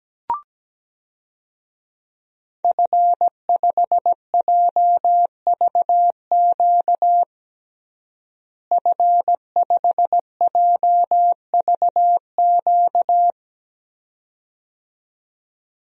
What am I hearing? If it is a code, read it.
Morse code, heard as a keyed tone, about 17 wpm, 700 Hz. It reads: F5JVQ F5JVQ
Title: DXCC-100 Call Signs - 5 Characters - Repeated 2x - 17wpm